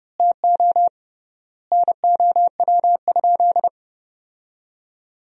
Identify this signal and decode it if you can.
Morse code, heard as a keyed tone, about 32 wpm, 700 Hz. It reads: TO NOW?
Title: 3x Word-Spacing - Sentences from Top 100 Words - 30wpm